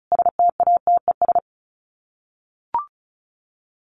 Morse code, heard as a keyed tone, about 34 wpm, 700 Hz. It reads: STATES
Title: Sets of 5 Words Encore2 35wpm